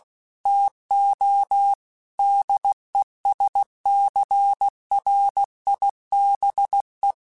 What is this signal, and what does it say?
Morse code, heard as a keyed tone, about 16 wpm, 800 Hz. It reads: TO DESCRIBE
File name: Technical Subjects / Technical Subjects-mp3s / Reactance @15